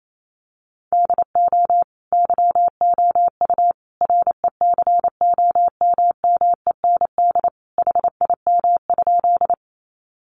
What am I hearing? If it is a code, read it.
Morse code, heard as a keyed tone, about 28 wpm, 700 Hz. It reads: DO YOU RECOMMEND HIM?
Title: Sentences from Top 1200 Words 28wpm